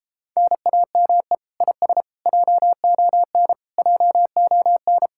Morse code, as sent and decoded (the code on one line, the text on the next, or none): -. .- -- . / .. ... / .--- --- -. / .--- --- -.
NAME IS JON JON